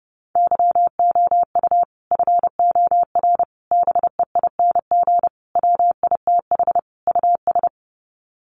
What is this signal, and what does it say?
Morse code, heard as a keyed tone, about 29 wpm, 700 Hz. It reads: YOU FOR BEING WITH US
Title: Sets of 5 Words 30wpm